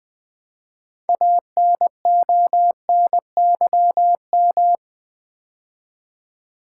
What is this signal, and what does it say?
Morse code, heard as a keyed tone, about 21 wpm, 700 Hz. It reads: ANONYM
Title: Build A Word - 3 Letters At A Time - 9-Letter Words - 20wpm